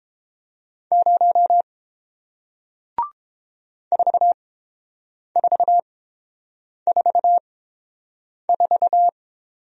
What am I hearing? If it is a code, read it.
Morse code, heard as a keyed tone, about 30 wpm, spaced as for 7 wpm, 700 Hz. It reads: 0 4444